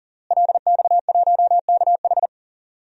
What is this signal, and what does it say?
Morse code, heard as a keyed tone, about 40 wpm, 700 Hz. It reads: LX1KH
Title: DXCC-100 Call Signs - 5 Characters - Repeated 3x - 40wpm